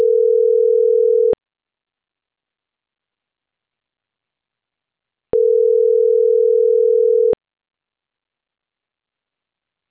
Ringback tone (North America), 440 + 480 Hz.